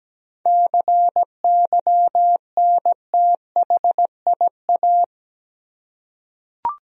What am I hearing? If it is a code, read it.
Morse code, heard as a keyed tone, about 17 wpm, 700 Hz. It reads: CYNTHIA
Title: USA Common Names - 17wpm